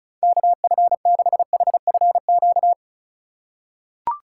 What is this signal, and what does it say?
Morse code, heard as a keyed tone, about 36 wpm, 700 Hz. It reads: KF6HFQ
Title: USA Call Signs - 6 Characters - 4x Character Spacing - 35wpm